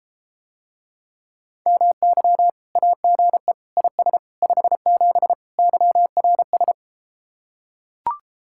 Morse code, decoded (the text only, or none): MY AGE IS 57 YRS